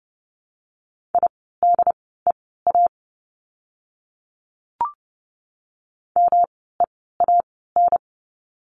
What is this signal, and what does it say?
Morse code, heard as a keyed tone, about 30 wpm, spaced as for 17 wpm, 700 Hz. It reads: IDEA MEAN
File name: course / 30:10wpm / 108 Lesson 18f - Letter m - Words 30wpm